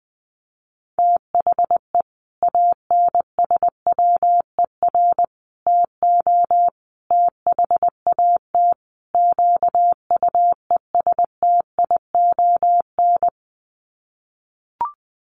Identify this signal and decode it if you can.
Morse code, heard as a keyed tone, about 20 wpm, 700 Hz. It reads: THE ANSWER TO THAT QUESTION
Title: Sets of 5 Words Encore2 20wpm